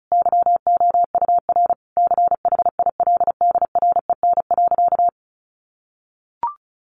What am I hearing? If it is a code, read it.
Morse code, heard as a keyed tone, about 34 wpm, 700 Hz. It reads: YOUR CHILDREN.
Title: Sentences from Top 1200 Words 35wpm